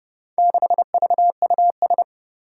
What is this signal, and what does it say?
Morse code, heard as a keyed tone, about 30 wpm, 700 Hz. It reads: 6VUS
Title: Four Challenging Letter-Number Combinations 30wpm